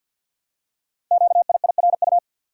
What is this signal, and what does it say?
Morse code, heard as a keyed tone, about 51 wpm, 700 Hz. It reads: QIIRU
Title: Code Groups - 5 Letters - 50wpm